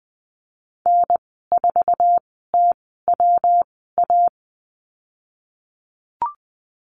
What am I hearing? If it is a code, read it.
Morse code, heard as a keyed tone, about 20 wpm, spaced as for 15 wpm, 700 Hz. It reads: N4TWA
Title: 154 Lesson 25g - Letter p - Call Signs 20wpm